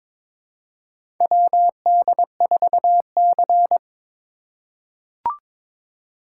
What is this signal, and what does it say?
Morse code, heard as a keyed tone, about 22 wpm, 700 Hz. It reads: WD4C
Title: USA Call Signs - 4 Characters - 3x Character Spacing - 22wpm